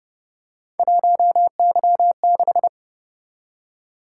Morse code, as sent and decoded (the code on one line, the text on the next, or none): .---- -.-- -....
1Y6